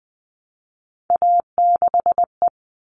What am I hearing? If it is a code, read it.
Morse code, heard as a keyed tone, about 20 wpm, 700 Hz. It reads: A6E
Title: All Cut Numbers - 3-Digit 20wpm